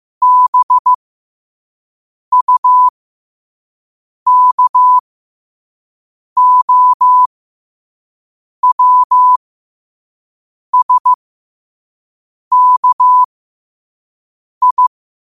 Morse code, heard as a keyed tone, about 15 wpm, spaced as for 5 wpm, 1000 Hz. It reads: BUKOWSKI
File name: qotdfiles / 5wpm